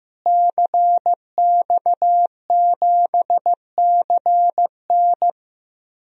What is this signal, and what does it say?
Morse code, heard as a keyed tone, about 15 wpm, 700 Hz. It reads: CX7CN
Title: DXCC-100 Call Signs - 5 Characters - Repeated 2x - 15wpm